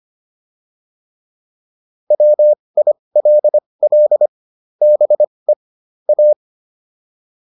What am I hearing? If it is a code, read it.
Morse code, heard as a keyed tone, about 25 wpm, spaced as for 20 wpm, 600 Hz. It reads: WILL BE A
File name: WD202 20